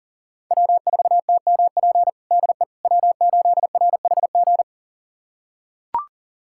Morse code, heard as a keyed tone, about 37 wpm, 700 Hz. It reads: W4TMP DE W8RHG
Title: USA Callsign DE Callsign - 5 Characters - 40wpm